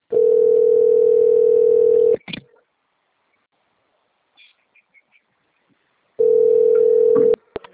Ringback tone (North America), 440 + 480 Hz.
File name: rg-1600-unknown- - - .11708